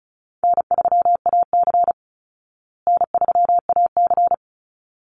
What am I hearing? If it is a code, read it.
Morse code, heard as a keyed tone, about 34 wpm, 700 Hz. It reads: N3AC N3AC
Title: Easy Call Signs Repeated 2x - 35wpm